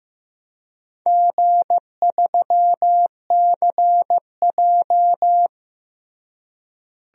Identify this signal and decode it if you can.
Morse code, heard as a keyed tone, about 15 wpm, 700 Hz. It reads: G3CJ
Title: DXCC-100 Call Signs - 4 Characters - Repeated 3x - 15wpm